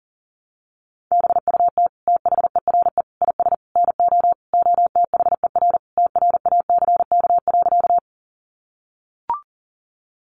Morse code, heard as a keyed tone, about 40 wpm, 700 Hz. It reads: BUT THERE IS NO OTHER TRACK.
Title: Sentences from Top 1100 Words 40wpm